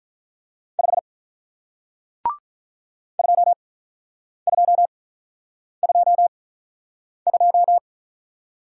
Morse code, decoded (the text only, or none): 5 2222